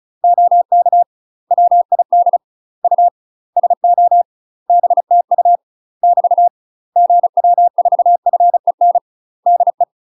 Morse code, heard as a keyed tone, about 35 wpm, 700 Hz. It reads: OK WID U SO BTU <BT> GW4FEN DE
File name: UK MASTER QSOs TRAINING / 35WPM / UK MASTER QSOs 0015